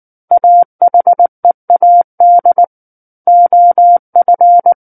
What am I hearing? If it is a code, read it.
Morse code, heard as a keyed tone, about 19 wpm, 700 Hz. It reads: AHEAD OF
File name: Book One / 10-Chapter 10 - 19 WPM